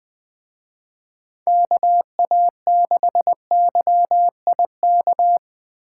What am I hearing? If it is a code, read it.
Morse code, heard as a keyed tone, about 20 wpm, 700 Hz. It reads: KA6YIK
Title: Speed Racing - USA 2x3 Call Signs 17wpm